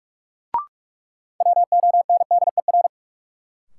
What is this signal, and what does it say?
Morse code, heard as a keyed tone, about 46 wpm, 700 Hz. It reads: WONDER
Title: Sentences from Top 500 Words - New Word Drill - 45wpm